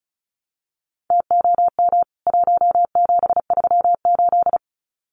Morse code, heard as a keyed tone, about 34 wpm, 700 Hz. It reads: TOM 1738
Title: CWT Exchange 35wpm